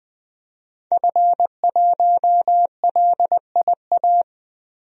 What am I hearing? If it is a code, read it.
Morse code, heard as a keyed tone, about 20 wpm, 700 Hz. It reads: F1LIA
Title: DXCC-100 Call Signs - 5 Characters - 20wpm